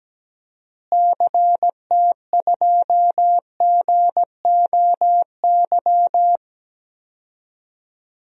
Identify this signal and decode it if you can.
Morse code, heard as a keyed tone, about 17 wpm, 700 Hz. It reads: CT2GOY
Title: DXCC-100 Call Signs - 6 Characters - Repeated 2x - 17wpm